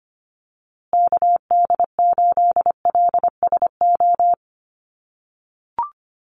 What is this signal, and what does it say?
Morse code, heard as a keyed tone, about 25 wpm, 700 Hz. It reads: KD8LSO
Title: USA Call Signs Repeated 2x 25wpm